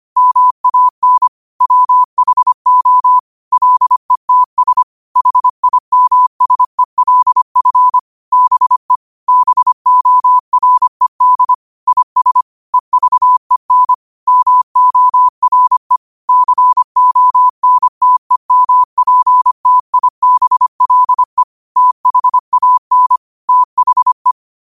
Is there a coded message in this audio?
Yes: Morse code from 0.2 to 24.3 s, heard as a keyed tone, about 25 wpm, 1000 Hz. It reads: MAN WHO LETS HIMSELF BE BORED IS EVEN MORE CONTEMPTIBLE THAN THE